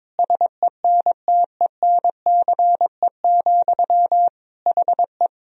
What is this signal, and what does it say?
Morse code, heard as a keyed tone, about 22 wpm, 700 Hz. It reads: SENTENCE, HE
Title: Sentences from Top 1000 Words - Review Part 1 - 22wpm